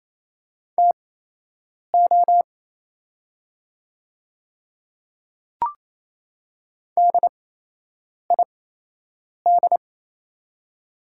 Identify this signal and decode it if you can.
Morse code, heard as a keyed tone, about 28 wpm, spaced as for 8 wpm, 700 Hz. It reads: TO DID
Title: 8x Character Spacing - Top 100 Words - 28wpm